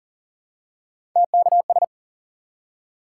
Morse code, heard as a keyed tone, about 40 wpm, 700 Hz. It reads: TKS
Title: Top 90 QSO Elements 40wpm